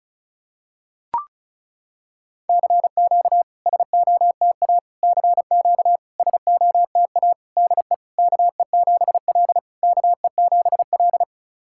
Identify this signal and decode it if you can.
Morse code, heard as a keyed tone, about 36 wpm, 690 Hz. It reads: CQ SOTA CQ SOTA DE KE7L KE7L
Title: SOTA QSO Practice 35wpm